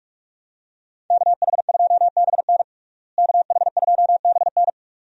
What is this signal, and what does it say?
Morse code, heard as a keyed tone, about 42 wpm, 700 Hz. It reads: KH2BN KH2BN
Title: USA Call Signs - 5 Characters - Repeated 2x - 45wpm